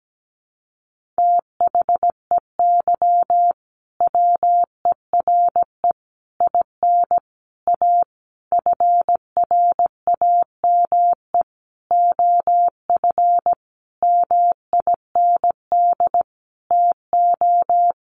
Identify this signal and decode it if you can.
Morse code, heard as a keyed tone, about 17 wpm, 700 Hz. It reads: THEY WERE IN A FRAME OF MIND TO
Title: Sentences from Top 1400 Words 17wpm